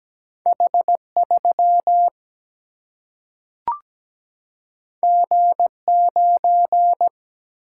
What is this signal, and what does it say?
Morse code, heard as a keyed tone, about 17 wpm, 700 Hz. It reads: H3 G9
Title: Single-Letter Single-Number 17wpm